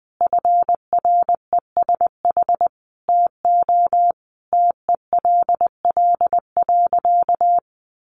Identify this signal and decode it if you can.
Morse code, heard as a keyed tone, about 20 wpm, 700 Hz. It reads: FRESH TO TELL.
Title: Sentences from Top 1200 Words 20wpm